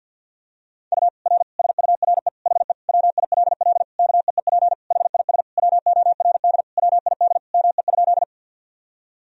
Morse code, heard as a keyed tone, about 52 wpm, 700 Hz. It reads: U R SURE HE WILL KEEP HIS WORD WID ME?
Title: Sentences from Top 300 Words and Abbreviations 50wpm